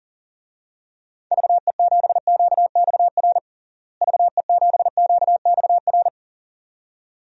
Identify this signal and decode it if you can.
Morse code, heard as a keyed tone, about 40 wpm, 700 Hz. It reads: VE7QXR VE7QXR
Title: DXCC-100 Call Signs - 6 Characters - Repeated 2x - 40wpm